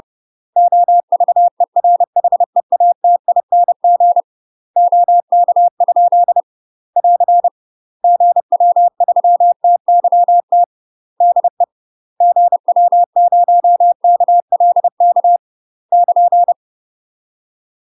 Morse code, heard as a keyed tone, about 30 wpm, 700 Hz. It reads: OVERHEATING OK? <AR> GW3TYT DE GW0KLK <KN>
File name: UK MASTER QSOs TRAINING / 30WPM / UK MASTER QSOs 0016